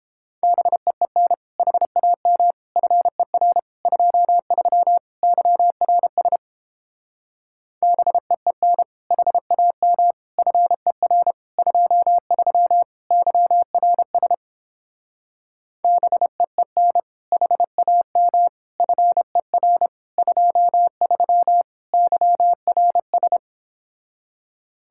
Morse code, heard as a keyed tone, about 29 wpm, 700 Hz. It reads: BEEN HAM FER 23 YRS BEEN HAM FER 23 YRS BEEN HAM FER 23 YRS